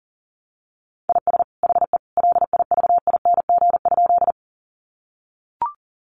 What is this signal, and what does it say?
Morse code, heard as a keyed tone, about 40 wpm, 700 Hz. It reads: IS HE LIVING?
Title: Sentences from Top 1100 Words 40wpm